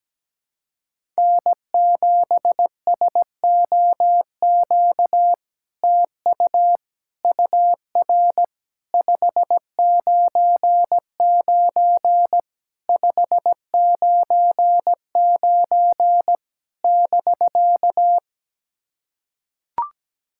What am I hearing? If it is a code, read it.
Morse code, heard as a keyed tone, about 17 wpm, 700 Hz. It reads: N7SOQ TU UR 599 599 <BK>
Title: SOTA QSO Practice 17wpm